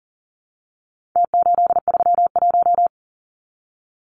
Morse code, heard as a keyed tone, about 40 wpm, 700 Hz. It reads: T831
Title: Numbers - Contest Style - 4-Digit 40wpm